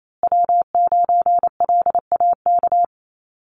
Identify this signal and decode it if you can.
Morse code, heard as a keyed tone, about 28 wpm, 700 Hz. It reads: W9LAK
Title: USA Call Signs Repeated 3x 28wpm